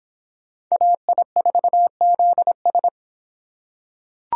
Morse code, heard as a keyed tone, about 25 wpm, 700 Hz. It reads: AI4ZS